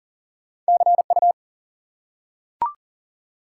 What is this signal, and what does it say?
Morse code, heard as a keyed tone, about 40 wpm, 700 Hz. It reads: CU
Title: Top 90 QSO Elements 40wpm